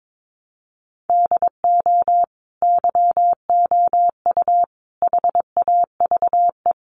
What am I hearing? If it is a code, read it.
Morse code, heard as a keyed tone, about 22 wpm, 700 Hz. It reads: DO YOU HAVE
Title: Sentences from Top 600 Words 22wpm